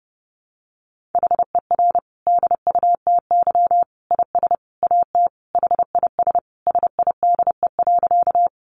Morse code, heard as a keyed tone, about 30 wpm, 700 Hz. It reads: HER DUTY IS AT HIS SIDE.
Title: Sentences from Top 1500 Words 30wpm